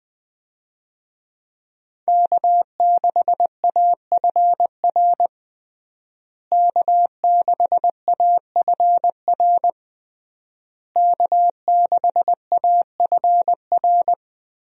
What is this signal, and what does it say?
Morse code, heard as a keyed tone, about 20 wpm, 700 Hz. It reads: K6AFR K6AFR K6AFR
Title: USA Call Signs - 5 Characters - Repeated 3x - 20wpm